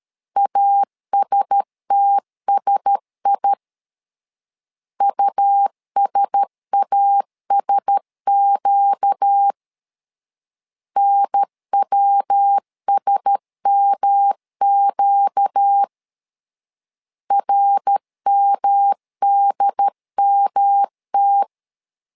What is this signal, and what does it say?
Morse code, heard as a keyed tone, about 13 wpm, 800 Hz. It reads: ASTSI USASQ NWSMQ RMDMT
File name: LEZIONI / 4 Q1S 12wpm 1802